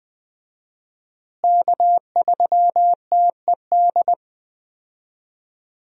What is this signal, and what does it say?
Morse code, heard as a keyed tone, about 21 wpm, 700 Hz. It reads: K3TED
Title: USA Call Signs - 5 Characters - Repeated 3x - 20wpm